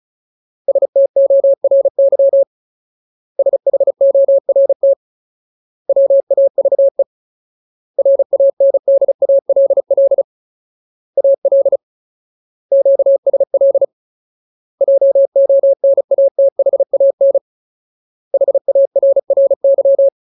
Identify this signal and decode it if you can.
Morse code, heard as a keyed tone, about 36 wpm, 550 Hz. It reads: STORY SHORT WAVE RANDALL AL QSL JONATHAN HARRY
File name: Ultimate Word List / 35WPM Ultimate